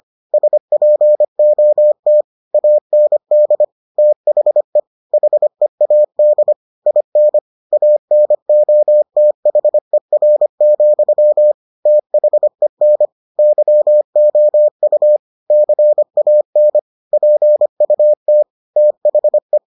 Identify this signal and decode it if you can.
Morse code, heard as a keyed tone, about 25 wpm, 600 Hz. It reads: SPOT AND THE HEAD IN ANOTHER, THEN YOU CAN PUT THE